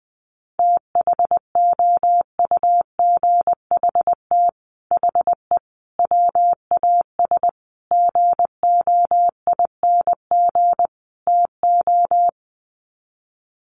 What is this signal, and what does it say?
Morse code, heard as a keyed tone, about 20 wpm, 700 Hz. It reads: THOUGHT HE WAS GOING TO
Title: Sets of 5 Words Encore2 20wpm